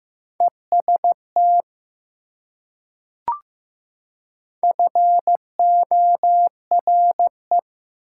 Morse code, heard as a keyed tone, about 15 wpm, 700 Hz. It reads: EST FORE
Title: Prefixes and Suffixes 15wpm